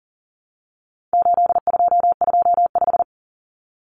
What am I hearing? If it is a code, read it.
Morse code, heard as a keyed tone, about 40 wpm, 700 Hz. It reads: 8225
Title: Common Cut Numbers - 4-Digit 40wpm